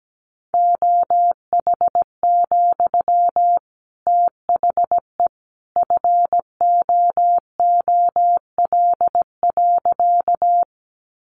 Morse code, heard as a keyed tone, about 17 wpm, 700 Hz. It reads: OH, THE FOOL.
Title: A Sound Of Thunder 17wpm